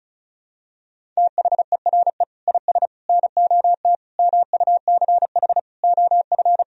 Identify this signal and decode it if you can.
Morse code, heard as a keyed tone, about 36 wpm, 700 Hz. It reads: THERE IS NOT MUCH OF
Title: Sentences from Top 1000 Words - Review Part 4 - 35wpm